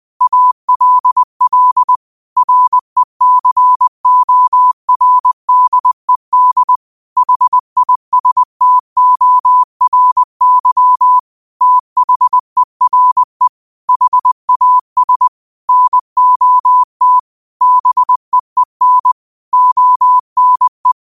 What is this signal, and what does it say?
Morse code, heard as a keyed tone, about 20 wpm, 1000 Hz. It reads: ALL RECORDED HISTORY THERE HAS NOT BEEN ONE